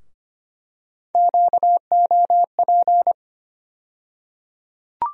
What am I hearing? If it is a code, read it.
Morse code, heard as a keyed tone, about 25 wpm, 700 Hz. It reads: QOP
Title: USA Call Sign Postfixes - 3 Letters 25wpm